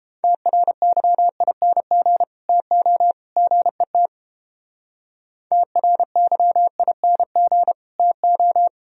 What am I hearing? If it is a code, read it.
Morse code, heard as a keyed tone, about 29 wpm, 700 Hz. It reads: TRYING TO GET TRYING TO